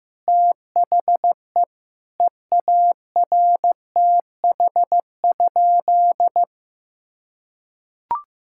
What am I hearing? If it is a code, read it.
Morse code, heard as a keyed tone, about 16 wpm, 700 Hz. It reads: THE EARTH?
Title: Sentences from Top 1600 Words 15wpm